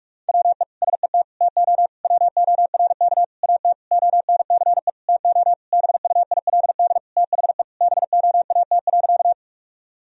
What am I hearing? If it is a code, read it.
Morse code, heard as a keyed tone, about 46 wpm, 700 Hz. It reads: WE SET TO WORK AT ONCE TO BUILD THE BOAT.
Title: Sentences from Top 1000 Words - Review Part 1 - 45wpm